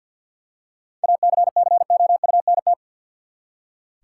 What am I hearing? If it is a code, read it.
Morse code, heard as a keyed tone, about 46 wpm, 700 Hz. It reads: ACCOUNT
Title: Seven Letter Words - 50wpm